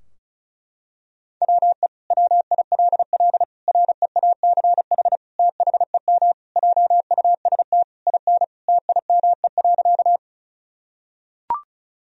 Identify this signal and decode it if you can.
Morse code, heard as a keyed tone, about 36 wpm, 700 Hz. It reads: WE WILL REACH THEM JUST IN TIME.
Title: Sentences from Top 1000 Words - Review Part 2 - 35wpm